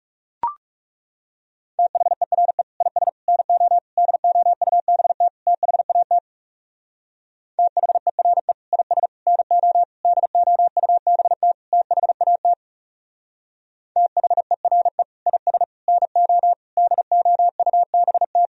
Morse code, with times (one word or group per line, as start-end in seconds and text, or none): 1.8-2.6 THERE
2.8-3.1 IS
3.3-3.8 NO
4.0-5.3 DOUBT
5.5-6.2 THAT
7.6-8.5 THERE
8.7-9.1 IS
9.3-9.8 NO
10.1-11.5 DOUBT
11.7-12.5 THAT
14.0-15.0 THERE
15.3-15.6 IS
15.9-16.5 NO
16.8-18.5 DOUBT